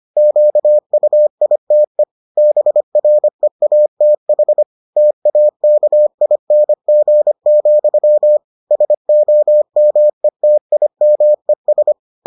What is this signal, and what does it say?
Morse code, heard as a keyed tone, about 25 wpm, 600 Hz. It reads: QUITE BREATH TAKING, SOMETIMES